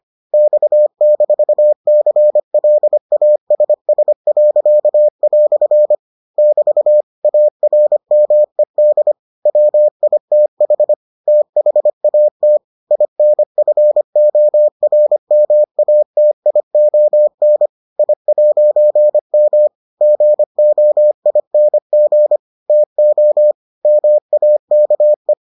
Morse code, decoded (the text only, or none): X-CLASS." <BT> ARMED WITH THAT INFORMATION I'M GOING TO MAKE